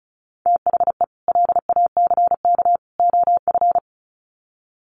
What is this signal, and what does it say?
Morse code, heard as a keyed tone, about 34 wpm, 700 Hz. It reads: THE LACK OF
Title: Sets of 3 Words 35wpm